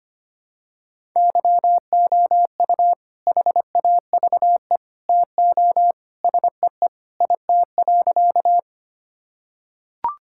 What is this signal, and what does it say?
Morse code, heard as a keyed tone, about 25 wpm, 700 Hz. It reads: YOU HAVE TO SEE IT.
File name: practice / 25wpm / Sentences with 2-Word Phrases - 25wpm